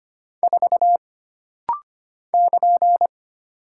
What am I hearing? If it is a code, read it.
Morse code, heard as a keyed tone, about 25 wpm, 700 Hz. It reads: 4 <KN>
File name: Single Character (Letter:Number:Punctuation:Prosign) - Rapid-Fire - 25wpm